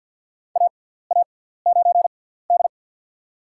Morse code, heard as a keyed tone, about 53 wpm, spaced as for 17 wpm, 700 Hz. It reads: AA9D
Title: USA Call Signs - 6x Character Spacing - 50wpm